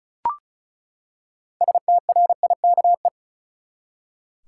Morse code, heard as a keyed tone, about 34 wpm, 700 Hz. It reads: STRIKE
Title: Six Letter Words - 35wpm